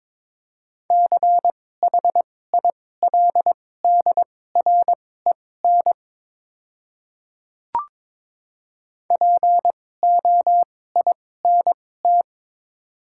Morse code, heard as a keyed tone, about 22 wpm, spaced as for 16 wpm, 700 Hz. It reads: CHILDREN POINT
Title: 2x Character Spacing - Top 500 Words - 22wpm